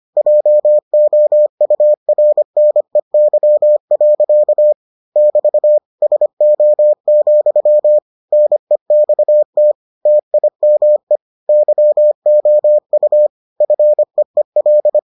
Morse code, heard as a keyed tone, about 25 wpm, 600 Hz. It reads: JOURNEY. <BT> SO, NEXT TIME YOU FEEL